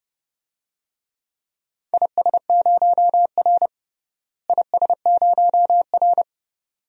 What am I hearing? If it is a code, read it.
Morse code, heard as a keyed tone, about 32 wpm, 700 Hz. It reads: IS0R IS0R
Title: DXCC-100 Call Signs - 4 Characters - Repeated 2x - 30wpm